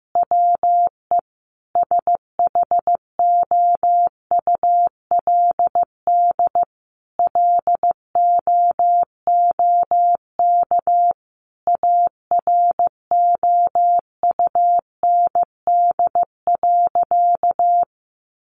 Morse code, heard as a keyed tone, about 15 wpm, 700 Hz. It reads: WE SHOULD LOOK AROUND.
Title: Sentences from Top 800 Words 15wpm